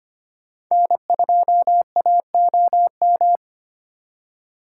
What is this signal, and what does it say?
Morse code, heard as a keyed tone, about 31 wpm, 700 Hz. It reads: N2AOM
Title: Speed Racing - USA Call Signs 25wpm